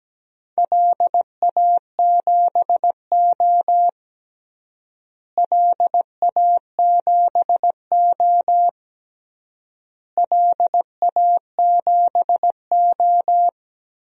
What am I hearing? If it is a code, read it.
Morse code, heard as a keyed tone, about 17 wpm, 700 Hz. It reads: LA7O LA7O LA7O
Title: DXCC-100 Call Signs - 4 Characters - Repeated 3x - 17wpm